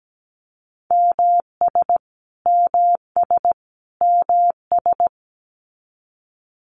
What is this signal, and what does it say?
Morse code, heard as a keyed tone, about 17 wpm, 700 Hz. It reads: MS MS MS
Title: US States Repeated 3x 17wpm